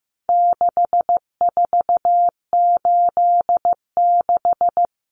Morse code, heard as a keyed tone, about 15 wpm, 700 Hz. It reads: 6486
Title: Numbers - 4-Digit 15wpm